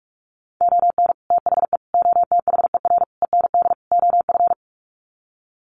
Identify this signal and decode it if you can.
Morse code, heard as a keyed tone, about 44 wpm, 700 Hz. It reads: ON THE OTHER END OF
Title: Sets of 5 Words Encore 45wpm